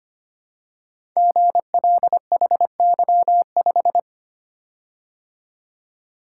Morse code, heard as a keyed tone, about 25 wpm, 700 Hz. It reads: GLHY5
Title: Code Groups - 5 Letter:Numbers - 25wpm